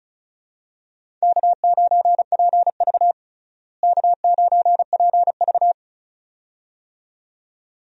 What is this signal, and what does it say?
Morse code, heard as a keyed tone, about 35 wpm, 700 Hz. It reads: K9PV K9PV
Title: USA Call Signs - 4 Characters - Repeated 2x - 35wpm